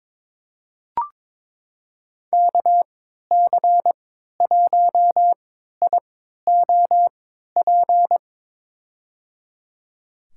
Morse code, heard as a keyed tone, about 23 wpm, spaced as for 13 wpm, 700 Hz. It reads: KC1IOP
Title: USA Call Signs - 3x Character Spacing - 22wpm